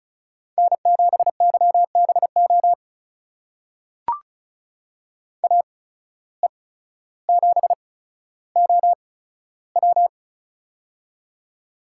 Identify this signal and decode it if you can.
Morse code, heard as a keyed tone, about 36 wpm, spaced as for 15 wpm, 700 Hz. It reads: N7YBO AE7OW
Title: USA Call Signs - 5 Characters - 8x Character Spacing - 35wpm